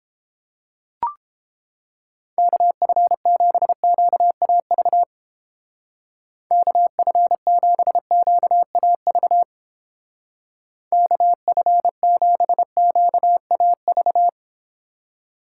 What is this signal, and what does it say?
Morse code, heard as a keyed tone, about 29 wpm, 700 Hz. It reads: KF7QAV KF7QAV KF7QAV